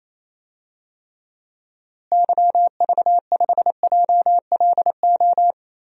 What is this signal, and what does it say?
Morse code, heard as a keyed tone, about 28 wpm, 700 Hz. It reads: YV5JLO
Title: DXCC-100 Call Signs - 6 Characters - Repeated 3x - 28wpm